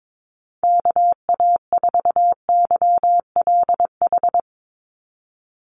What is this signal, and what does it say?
Morse code, heard as a keyed tone, about 22 wpm, 700 Hz. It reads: KA4YLH
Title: USA Call Signs 22wpm